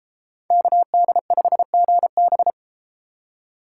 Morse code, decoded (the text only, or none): KD5GB